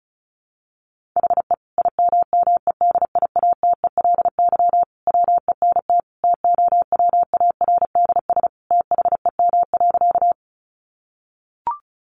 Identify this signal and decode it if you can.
Morse code, heard as a keyed tone, about 34 wpm, 700 Hz. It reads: HE IMMEDIATELY WENT TOWARDS THEM.
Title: Sentences from Top 1100 Words 35wpm